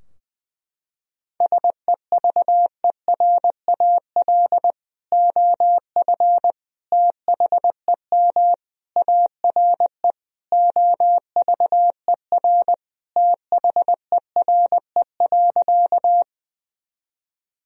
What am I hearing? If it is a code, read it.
Morse code, heard as a keyed tone, about 20 wpm, 700 Hz. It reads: SEVERAL OF THEM ARE OVER THERE.
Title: Sentences from Top 1000 Words - Review Part 2 - 20wpm